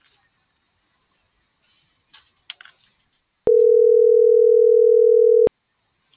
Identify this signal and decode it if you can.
Ringback tone (North America), 440 + 480 Hz.